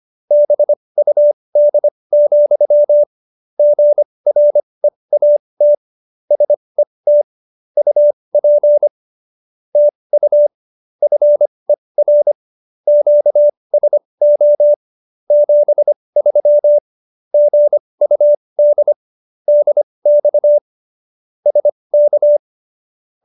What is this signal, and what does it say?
Morse code, heard as a keyed tone, about 25 wpm, spaced as for 20 wpm, 590 Hz. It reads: BUD, GREAT SET UP TU FER QSO 73 GUD DX SK
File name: QSO302 20